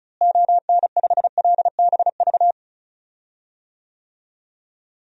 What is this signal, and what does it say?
Morse code, heard as a keyed tone, about 34 wpm, 700 Hz. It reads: ON5LBV
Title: DXCC-100 Call Signs - 6 Characters - Repeated 2x - 35wpm